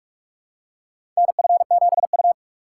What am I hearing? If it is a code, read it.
Morse code, heard as a keyed tone, about 46 wpm, 700 Hz. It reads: NF7U
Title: USA Call Signs - 4 Characters - 3x Character Spacing - 45wpm